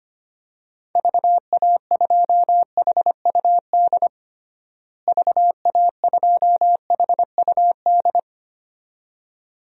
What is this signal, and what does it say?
Morse code, heard as a keyed tone, about 25 wpm, 700 Hz. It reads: VA2HUD VA2HUD
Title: DXCC-100 Call Signs - 6 Characters - Repeated 2x - 25wpm